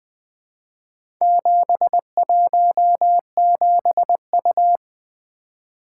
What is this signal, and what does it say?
Morse code, heard as a keyed tone, about 20 wpm, 700 Hz. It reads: 717U
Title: Four Letter-Number Combinations - 20wpm